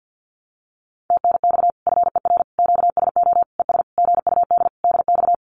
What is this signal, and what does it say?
Morse code, heard as a keyed tone, about 50 wpm, 700 Hz. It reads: TNX FER QSO ES GUD DX
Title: Common QSO Phrases Encore 50wpm